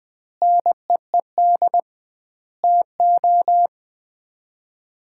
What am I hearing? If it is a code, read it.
Morse code, heard as a keyed tone, about 20 wpm, 700 Hz. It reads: NEED TO
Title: 2x Word-Spacing - Sets of 5 Words - 20wpm